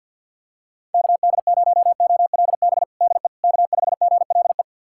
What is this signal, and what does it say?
Morse code, heard as a keyed tone, about 50 wpm, 700 Hz. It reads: KD0OLB DE K5GLE
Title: USA Callsign DE Callsign 50wpm